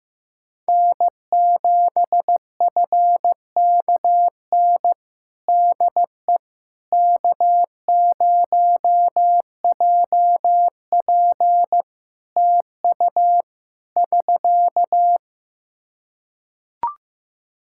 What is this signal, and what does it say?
Morse code, heard as a keyed tone, about 15 wpm, 700 Hz. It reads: N7FKN DE K0JP TU <SK>
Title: Practice QSO - Exchange 4 - Ending - 15wpm